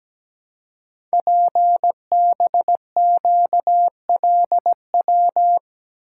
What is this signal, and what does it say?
Morse code, heard as a keyed tone, about 17 wpm, 700 Hz. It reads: PBQLW
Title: Code Groups - 5 Letters - 17wpm